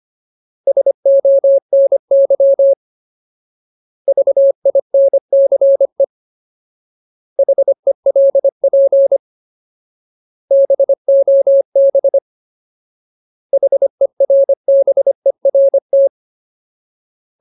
Morse code, heard as a keyed tone, about 25 wpm, 550 Hz. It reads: SONY VINCE HELP BOB HERBERT
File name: Ultimate Word List / 25WPM Ultimate